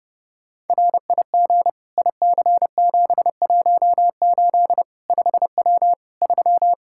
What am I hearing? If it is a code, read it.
Morse code, heard as a keyed tone, about 32 wpm, 700 Hz. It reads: RIG IC718 5W 3
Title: K1USN SST QSO Practice 30wpm